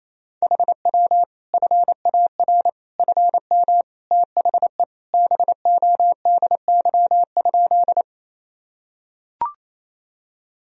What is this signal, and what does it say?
Morse code, heard as a keyed tone, about 28 wpm, 700 Hz. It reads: HW FAR FM THE BODY?
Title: Sentences from Top 400 Words and Abbreviations 28wpm